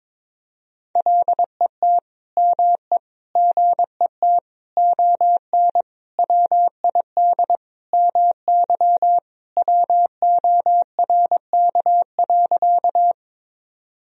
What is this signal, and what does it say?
Morse code, heard as a keyed tone, about 21 wpm, 700 Hz. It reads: LET ME GET ON WID MY WORK.
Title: Sentences from Top 300 Words and Abbreviations 22wpm